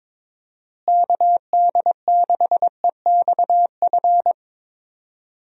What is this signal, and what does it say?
Morse code, heard as a keyed tone, about 22 wpm, 700 Hz. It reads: KD6EXF
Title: USA Call Signs - 6 Characters - Repeated 3x - 22wpm